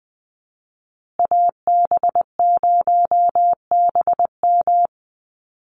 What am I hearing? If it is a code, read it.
Morse code, heard as a keyed tone, about 20 wpm, 700 Hz. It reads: AB0BM
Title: USA Call Signs Repeated 2x 20wpm